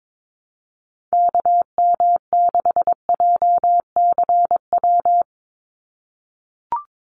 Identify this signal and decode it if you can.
Morse code, heard as a keyed tone, about 22 wpm, 700 Hz. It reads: KM6JCW
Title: USA Call Signs 22wpm